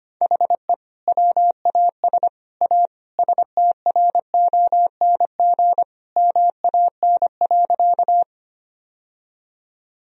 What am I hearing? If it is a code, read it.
Morse code, heard as a keyed tone, about 25 wpm, 700 Hz. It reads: HE WAS A STRONG MAN.
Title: Sentences from Top 1000 Words - Review Part 3 - 25wpm